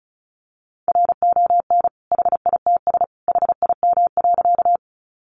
Morse code, heard as a keyed tone, about 34 wpm, 700 Hz. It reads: RON HITS HIM.
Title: Additional Easy Phrases 35wpm